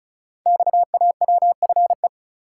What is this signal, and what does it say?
Morse code, heard as a keyed tone, about 36 wpm, 700 Hz. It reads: XAWFE
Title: Code Groups - 5 Letters - 35wpm